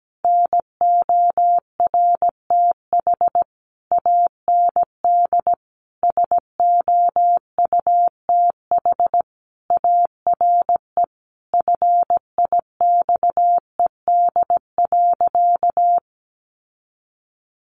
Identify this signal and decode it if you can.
Morse code, heard as a keyed tone, about 17 wpm, 700 Hz. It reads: NORTH AND SOUTH ARE FIXED.
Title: Sentences from Top 1100 Words 17wpm